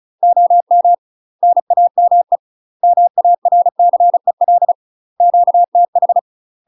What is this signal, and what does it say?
Morse code, heard as a keyed tone, about 35 wpm, 700 Hz. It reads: OM NAME MARCEL QTH
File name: UK MASTER QSOs TRAINING / 35WPM / UK MASTER QSOs 0020